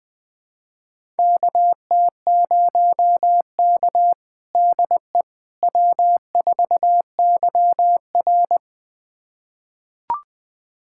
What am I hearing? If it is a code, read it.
Morse code, heard as a keyed tone, about 20 wpm, 700 Hz. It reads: KT0K DE W4YR
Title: USA Callsign DE Callsign - 4 Characters - 20wpm